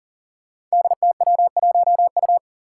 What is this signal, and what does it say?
Morse code, heard as a keyed tone, about 40 wpm, 700 Hz. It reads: DTW1U
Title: Five Challenging Letter-Number Combinations 40wpm